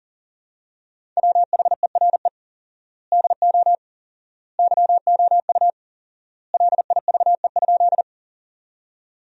Morse code, heard as a keyed tone, about 40 wpm, 700 Hz. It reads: WHERE DO YOU LIVE?
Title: 4x Word-Spacing - Sentences from Top 200 Words - 40wpm